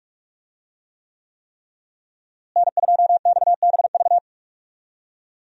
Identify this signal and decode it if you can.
Morse code, heard as a keyed tone, about 46 wpm, 700 Hz. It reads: N2XBV
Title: USA Call Signs - 5 Characters - Repeated 3x - 45wpm